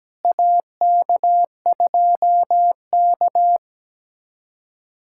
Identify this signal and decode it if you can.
Morse code, heard as a keyed tone, about 17 wpm, 700 Hz. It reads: AK2K
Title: USA Callsign DE Callsign - 4 Characters - 17wpm